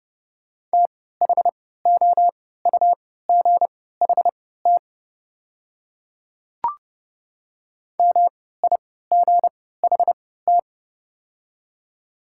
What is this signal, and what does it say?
Morse code, heard as a keyed tone, about 31 wpm, spaced as for 17 wpm, 700 Hz. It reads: THOUGHT MIGHT
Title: 3x Character Spacing - Top 300 Words - 30wpm